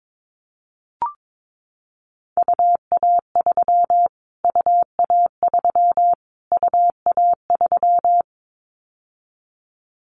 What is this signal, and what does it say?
Morse code, heard as a keyed tone, about 22 wpm, 700 Hz. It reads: UA3 UA3 UA3
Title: Callsign Prefixes for Top 25 DXCC - Repeated 3x - 22wpm